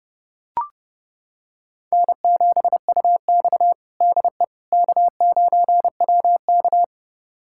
Morse code, heard as a keyed tone, about 32 wpm, 700 Hz. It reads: N7UX DE K9WK
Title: USA Callsign DE Callsign - 4 Characters - 30wpm